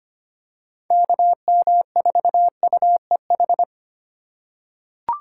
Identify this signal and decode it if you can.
Morse code, heard as a keyed tone, about 25 wpm, 700 Hz. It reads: KM4UEH
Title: USA Call Signs - 4x Character Spacing - 25wpm